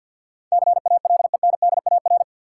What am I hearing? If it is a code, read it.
Morse code, heard as a keyed tone, about 51 wpm, 700 Hz. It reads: CALENDAR
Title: Sentences from Top 1600 Words - New Word Drill Alt - 50wpm